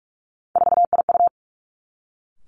Morse code, heard as a keyed tone, about 46 wpm, 700 Hz. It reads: 4IU
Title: Three Challenging Letter-Number Combinations 45wpm